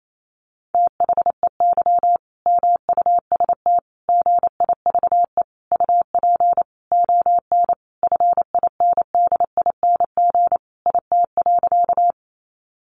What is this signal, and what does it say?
Morse code, heard as a keyed tone, about 28 wpm, 700 Hz. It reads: THEY MUST GIVE UP ON FINDING IT.
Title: Sentences from Top 1100 Words 28wpm